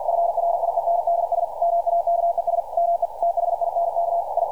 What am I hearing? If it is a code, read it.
Morse code, heard as a keyed tone, about 50 wpm, 690 Hz, with radio noise. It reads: FRIENDS MAKES ME THINK I
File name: archived / CRSnet